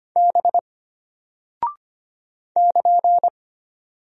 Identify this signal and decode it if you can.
Morse code, heard as a keyed tone, about 25 wpm, 700 Hz. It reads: B <KN>
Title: Single Character (Letter:Number:Punctuation:Prosign) Alt 25wpm